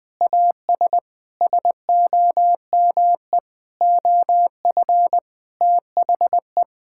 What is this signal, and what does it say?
Morse code, heard as a keyed tone, about 20 wpm, 700 Hz. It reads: AS SOME OF THE
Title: Sentences with 6-Word Phrases - 20wpm